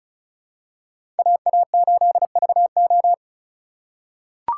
Morse code, heard as a keyed tone, about 36 wpm, 700 Hz. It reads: AA8VO
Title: USA Call Signs - 5 Characters - 3x Character Spacing - 35wpm